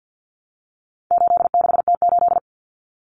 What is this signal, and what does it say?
Morse code, heard as a keyed tone, about 50 wpm, 700 Hz. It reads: 86T8
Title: Common Cut Numbers - 4-Digit 50wpm